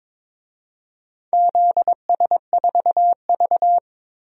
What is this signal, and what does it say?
Morse code, heard as a keyed tone, about 22 wpm, 700 Hz. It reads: ZS4V
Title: DXCC-100 Call Signs - 4 Characters - Repeated 3x - 22wpm